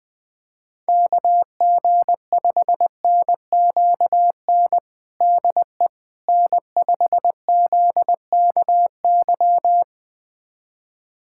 Morse code, heard as a keyed tone, about 21 wpm, 700 Hz. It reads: KG5NQN DE N5ZKY
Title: USA Callsign DE Callsign 20wpm